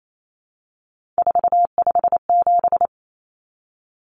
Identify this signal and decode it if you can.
Morse code, heard as a keyed tone, about 28 wpm, 700 Hz. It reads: 457
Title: Numbers - 3-Digit 28wpm